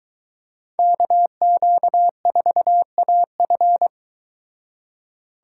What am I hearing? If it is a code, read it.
Morse code, heard as a keyed tone, about 23 wpm, 700 Hz. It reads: KQ4AF